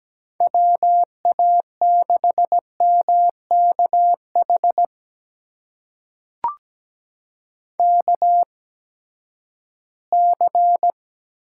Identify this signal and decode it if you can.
Morse code, heard as a keyed tone, about 17 wpm, spaced as for 12 wpm, 700 Hz. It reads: WA6MKH KC
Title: USA Call Signs - 6 Characters - 8x Character Spacing - 17wpm